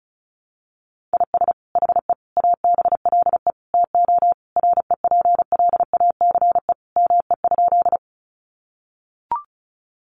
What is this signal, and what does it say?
Morse code, heard as a keyed tone, about 34 wpm, 700 Hz. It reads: IS HE ABLE TO REPLACE ME?
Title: Sentences from Top 1400 Words 35wpm